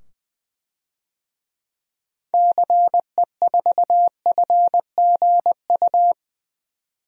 Morse code, heard as a keyed tone, about 20 wpm, 700 Hz. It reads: CE4FGU
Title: DXCC-100 Call Signs - 6 Characters - 20wpm